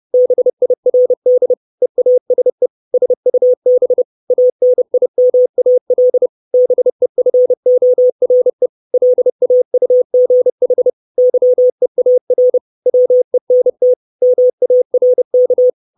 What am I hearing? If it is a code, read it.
Morse code, heard as a keyed tone, about 30 wpm, 500 Hz. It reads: BIRD EASE SUB ANIMAL BEFORE LAUGH YEAR WENT MARK
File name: 632 Words 30WPM